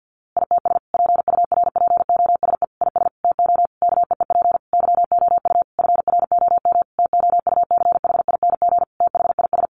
Morse code, heard as a keyed tone, about 49 wpm, 700 Hz. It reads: ITS PURPOSE IS TO KEEP YOU FROM TOUCHING THIS
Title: A Sound Of Thunder 50wpm